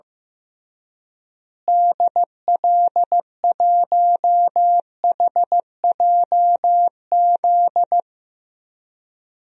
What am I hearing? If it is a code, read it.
Morse code, heard as a keyed tone, about 16 wpm, 700 Hz. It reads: DL1HJZ
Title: DXCC-100 Call Signs - 6 Characters - Repeated 3x - 15wpm